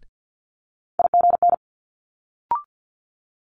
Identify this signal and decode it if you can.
Morse code, heard as a keyed tone, about 50 wpm, 700 Hz. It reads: IGN
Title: Multi-letter Phonemes Quick Fire 50wpm